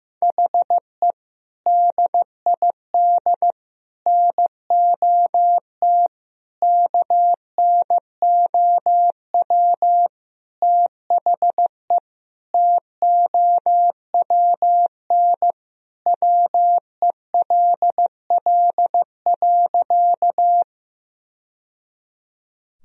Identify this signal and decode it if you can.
Morse code, heard as a keyed tone, about 16 wpm, 700 Hz. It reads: HE DID NOT KNOW THE TOWN WELL.
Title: Sentences from Top 1000 Words - Review Part 2 - 15wpm